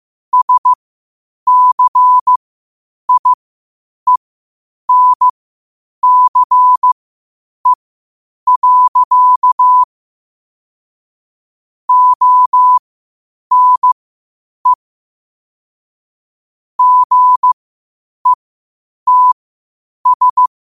Morse code, heard as a keyed tone, about 15 wpm, spaced as for 8 wpm, 1000 Hz. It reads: SCIENCE. ONE GETS
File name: qotdfiles / 8wpm